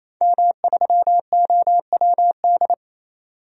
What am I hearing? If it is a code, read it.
Morse code, heard as a keyed tone, about 28 wpm, 700 Hz. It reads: M3OWD
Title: DXCC-100 Call Signs - 5 Characters - 28wpm